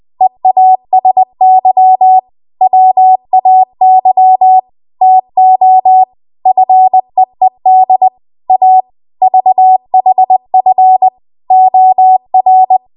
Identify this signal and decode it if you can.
Morse code, heard as a keyed tone, about 20 wpm, 750 Hz. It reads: EASY WAY TO FEED A VHF OR
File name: Archive / 20 WPM / 20WPM